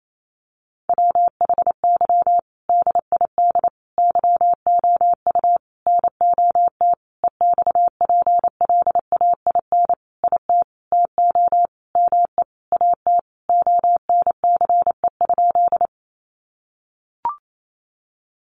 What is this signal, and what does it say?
Morse code, heard as a keyed tone, about 28 wpm, 700 Hz. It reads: WHY DID YOU NOT EXPLAIN IT TO ME AT ONCE?
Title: Sentences from Top 1100 Words 28wpm